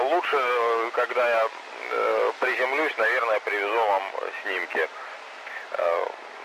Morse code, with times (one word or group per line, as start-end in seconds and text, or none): none